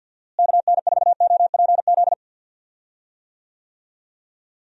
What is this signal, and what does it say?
Morse code, heard as a keyed tone, about 48 wpm, 700 Hz. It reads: KN4OPZ
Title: USA Call Signs - 6 Characters - Repeated 2x - 50wpm